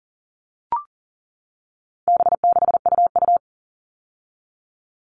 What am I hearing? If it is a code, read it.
Morse code, heard as a keyed tone, about 40 wpm, 700 Hz. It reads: B6UU
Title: All Cut Numbers - 4-Digit 40wpm